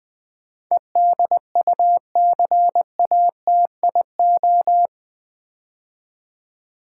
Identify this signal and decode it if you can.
Morse code, heard as a keyed tone, about 20 wpm, 700 Hz. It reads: EDUCATIO
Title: Build A Word - 4 Letters At A Time - 11-Letter Words - 20wpm